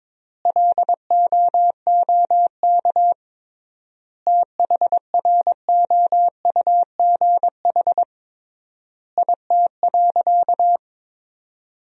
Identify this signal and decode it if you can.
Morse code, heard as a keyed tone, about 22 wpm, 700 Hz. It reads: LOOK THROUGH IT.
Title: 3x Word-Spacing - Sentences from Top 200 Words - 22wpm